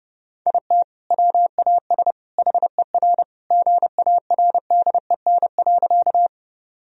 Morse code, heard as a keyed tone, about 30 wpm, 700 Hz. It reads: IT WAS HER GARDEN.
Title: Sentences from Top 1000 Words - Review Part 4 - 30wpm